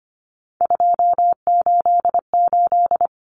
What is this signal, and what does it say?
Morse code, heard as a keyed tone, about 25 wpm, 700 Hz. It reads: 288
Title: Common Cut Numbers - 3-Digit 25wpm